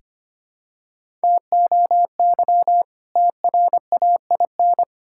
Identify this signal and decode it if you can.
Morse code, heard as a keyed tone, about 25 wpm, 700 Hz. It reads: TOY TRAIN
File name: practice / 25wpm / Difficult Words with Hints - 25wpm